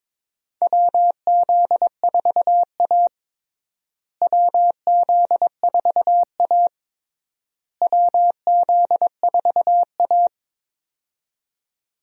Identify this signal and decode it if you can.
Morse code, heard as a keyed tone, about 23 wpm, 700 Hz. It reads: WZ4A WZ4A WZ4A
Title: USA Call Signs - 4 Characters - Repeated 3x - 22wpm